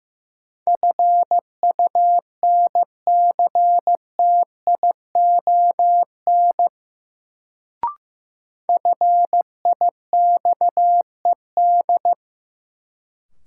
Morse code, heard as a keyed tone, about 15 wpm, 700 Hz. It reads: FUNCTION FIXED
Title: Sentences from Top 1100 Words - New Word Drill - 15wpm